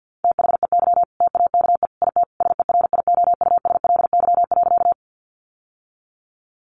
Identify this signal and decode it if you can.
Morse code, heard as a keyed tone, about 50 wpm, 700 Hz. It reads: THEY TAKE IT SERIOUSLY.
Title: Sentences from Top 1200 Words 50wpm